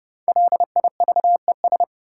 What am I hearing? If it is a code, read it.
Morse code, heard as a keyed tone, about 30 wpm, 700 Hz. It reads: LIVES
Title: Sets of 5 Words Encore - New Word Drill - 30wpm